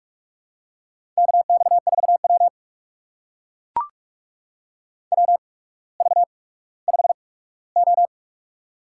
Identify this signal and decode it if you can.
Morse code, heard as a keyed tone, about 44 wpm, spaced as for 19 wpm, 690 Hz. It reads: KX4W WV5O
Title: USA Call Signs - 4 Characters - 8x Character Spacing - 45wpm